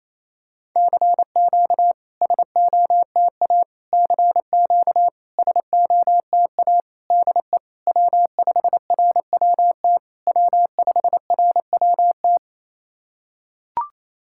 Morse code, heard as a keyed tone, about 27 wpm, 690 Hz. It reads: CQ SOTA CQ SOTA DE W5RWT W5RWT
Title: SOTA QSO Practice 28wpm